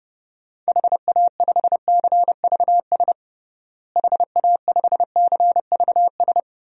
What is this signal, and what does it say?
Morse code, heard as a keyed tone, about 32 wpm, 700 Hz. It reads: HA5CVS HA5CVS
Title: DXCC-100 Call Signs - 6 Characters - Repeated 2x - 30wpm